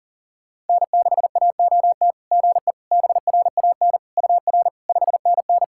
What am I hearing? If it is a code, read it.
Morse code, heard as a keyed tone, about 40 wpm, 660 Hz. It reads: N6AOT GE BRAN UR 5NN
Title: SOTA QSO Practice 40wpm